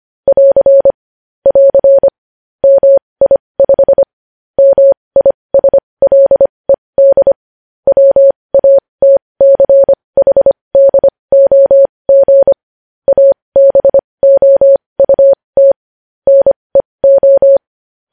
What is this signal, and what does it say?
Morse code, heard as a keyed tone, about 25 wpm, spaced as for 20 wpm, 550 Hz. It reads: <AR> <AR> MI5 MISLED WATCHDOG ABOUT NEO
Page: News headlines in Morse code at 20 wpm